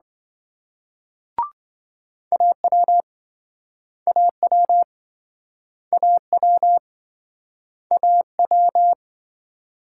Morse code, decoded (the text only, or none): AW AW AW AW